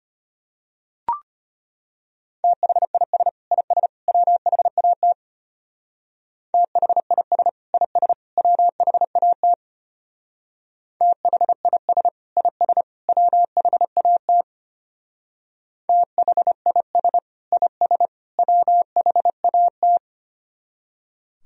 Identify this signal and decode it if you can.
Morse code, heard as a keyed tone, about 32 wpm, 700 Hz. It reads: THIS IS WHAT THIS IS WHAT THIS IS WHAT THIS IS WHAT